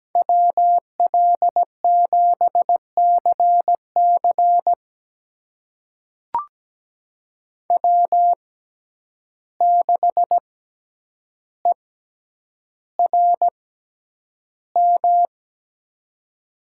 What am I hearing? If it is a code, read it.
Morse code, heard as a keyed tone, about 17 wpm, spaced as for 9 wpm, 700 Hz. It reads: WL7CC W6ERM
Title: USA Call Signs - 5 Characters - 6x Character Spacing - 17wpm